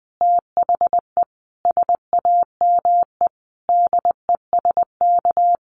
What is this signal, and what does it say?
Morse code, heard as a keyed tone, about 20 wpm, 700 Hz. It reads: THE SAME DESK
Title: A Sound Of Thunder 20wpm